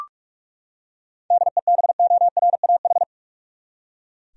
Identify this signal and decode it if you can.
Morse code, heard as a keyed tone, about 44 wpm, 700 Hz. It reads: DEBORAH
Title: USA Common Names - 45wpm